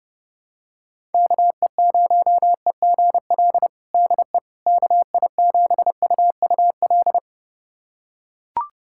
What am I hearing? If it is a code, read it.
Morse code, heard as a keyed tone, about 30 wpm, 700 Hz. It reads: KE0EGL DE KI7UUL
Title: USA Callsign DE Callsign 30wpm